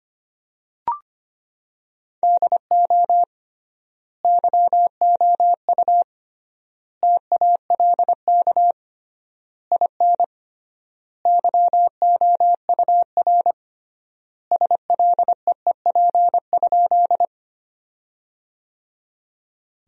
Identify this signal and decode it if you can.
Morse code, heard as a keyed tone, about 25 wpm, 700 Hz. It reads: DO YOU TALK IN YOUR SLEEP?
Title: 3x Word-Spacing - Sentences from Top 500 Words - 25wpm